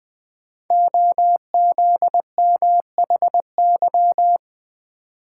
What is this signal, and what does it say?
Morse code, heard as a keyed tone, about 20 wpm, 700 Hz. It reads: OZMHY
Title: Code Groups - 5 Letters - 20wpm